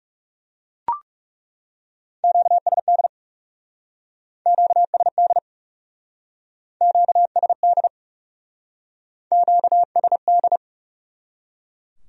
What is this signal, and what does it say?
Morse code, heard as a keyed tone, about 40 wpm, 700 Hz. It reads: QSD QSD QSD QSD